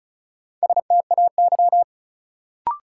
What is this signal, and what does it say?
Morse code, heard as a keyed tone, about 34 wpm, 700 Hz. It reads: STAY
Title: Sentences from Top 500 Words - New Word Drill - 35wpm